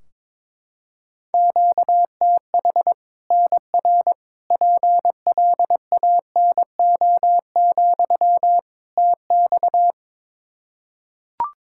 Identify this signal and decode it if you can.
Morse code, heard as a keyed tone, about 22 wpm, 700 Hz. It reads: QTH NR PLANO, TX
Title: Practice QSO - Exchange 1 - Beginning - 22wpm